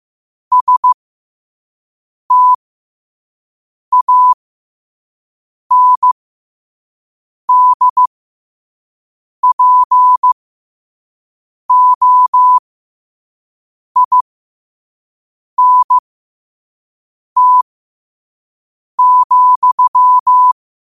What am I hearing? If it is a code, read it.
Morse code, heard as a keyed tone, about 15 wpm, spaced as for 5 wpm, 1000 Hz. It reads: STANDPOINT,
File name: qotdfiles / 5wpm